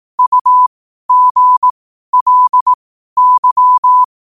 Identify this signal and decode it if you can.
Morse code, heard as a keyed tone, about 18 wpm, spaced as for 12 wpm, 1000 Hz. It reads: UGLY